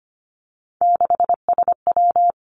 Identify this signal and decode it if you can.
Morse code, heard as a keyed tone, about 25 wpm, 700 Hz. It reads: 6SW
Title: Three Challenging Letter-Number Combinations 25wpm